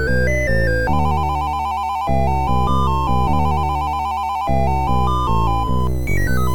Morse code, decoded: none